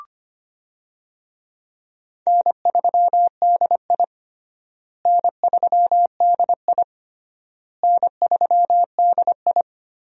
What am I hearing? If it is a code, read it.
Morse code, heard as a keyed tone, about 25 wpm, 700 Hz. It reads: N3DI N3DI N3DI
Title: USA Call Signs - 4 Characters - Repeated 3x - 25wpm